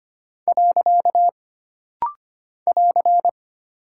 Morse code, heard as a keyed tone, about 25 wpm, 700 Hz. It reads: . <AR>
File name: Single Punctuation-Prosign - Rapid-Fire - 25wpm